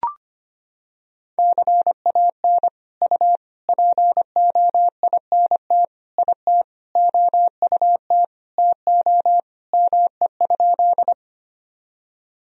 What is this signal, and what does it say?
Morse code, heard as a keyed tone, about 25 wpm, 700 Hz. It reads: CAN U POINT IT OUT TO ME?
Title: Sentences from Top 200 Words and Abbreviations 25wpm